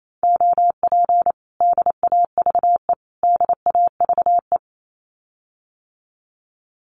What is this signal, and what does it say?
Morse code, heard as a keyed tone, about 28 wpm, 700 Hz. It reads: OP DAVE DAVE
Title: Common QSO Phrases Encore 28wpm